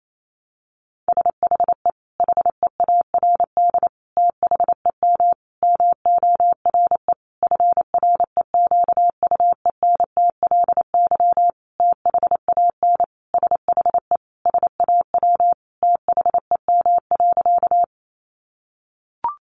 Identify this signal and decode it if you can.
Morse code, heard as a keyed tone, about 28 wpm, 700 Hz. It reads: SHE HEARD THEM MORE FREQUENTLY THAN SHE SAW THEM.
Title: Sentences from Top 1300 Words 28wpm